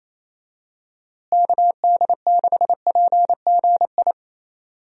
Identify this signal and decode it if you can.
Morse code, heard as a keyed tone, about 28 wpm, 700 Hz. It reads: KD6PGI
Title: USA Call Signs - 6 Characters - Repeated 3x - 28wpm